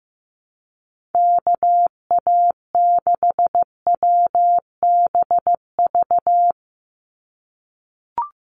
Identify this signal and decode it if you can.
Morse code, heard as a keyed tone, about 15 wpm, 700 Hz. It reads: KA6WBV
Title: USA Call Signs 15wpm